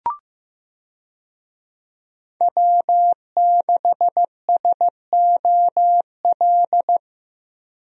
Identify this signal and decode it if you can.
Morse code, heard as a keyed tone, about 15 wpm, 700 Hz. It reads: W6SOL
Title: USA Call Signs - 5 Characters - Repeated 3x - 15wpm